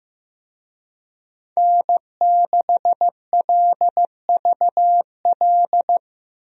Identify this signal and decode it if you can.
Morse code, heard as a keyed tone, about 16 wpm, 700 Hz. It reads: N6LVL
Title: USA Call Signs - 5 Characters - Repeated 2x - 15wpm